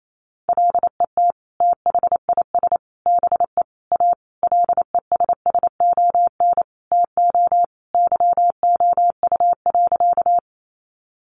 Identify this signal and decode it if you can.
Morse code, heard as a keyed tone, about 28 wpm, 700 Hz. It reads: LET THIS BE A LESSON TO YOU.
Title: Sentences from Top 1200 Words 28wpm